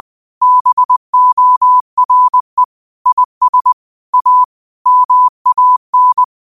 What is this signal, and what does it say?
Morse code, heard as a keyed tone, about 20 wpm, 1000 Hz. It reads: BORE IS A MAN
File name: qotdfiles / 20wpm